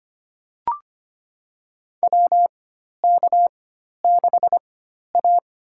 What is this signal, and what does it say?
Morse code, heard as a keyed tone, about 25 wpm, spaced as for 12 wpm, 700 Hz. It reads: WK6A
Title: USA Call Signs - 4 Characters - 4x Character Spacing - 25wpm